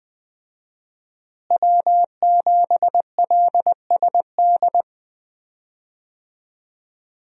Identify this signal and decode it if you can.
Morse code, heard as a keyed tone, about 20 wpm, 700 Hz. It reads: W7LSD
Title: USA Call Signs - 5 Characters - Repeated 3x - 20wpm